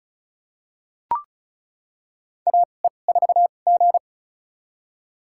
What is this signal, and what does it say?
Morse code, heard as a keyed tone, about 36 wpm, spaced as for 26 wpm, 700 Hz. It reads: AE4G
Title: USA Call Signs - 4 Characters - 2x Character Spacing - 35wpm